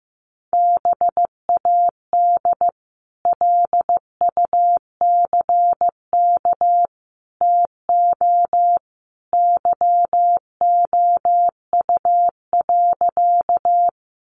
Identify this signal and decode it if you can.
Morse code, heard as a keyed tone, about 15 wpm, 700 Hz. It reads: BAD LUCK TO YOU.
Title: Sentences from Top 1500 Words 15wpm